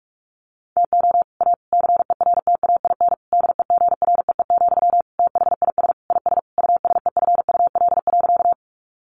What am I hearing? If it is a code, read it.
Morse code, heard as a keyed tone, about 44 wpm, 700 Hz. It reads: TO A CERTAIN DEGREE, THIS IS USEFUL.
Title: Sentences from Top 1500 Words 45wpm